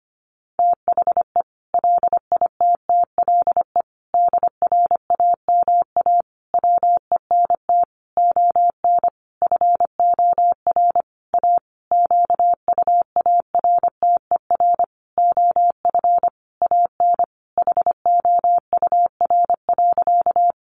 Morse code, heard as a keyed tone, about 25 wpm, 700 Hz. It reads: THE LITTLE DRAMA WENT ON FOR A QUARTER OF AN HOUR.
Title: Sentences from Top 1500 Words 25wpm